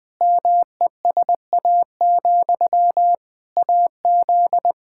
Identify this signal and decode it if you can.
Morse code, heard as a keyed tone, about 20 wpm, 700 Hz. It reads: MESA, AZ
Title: Practice QSO - Exchange 1 - Beginning - 20wpm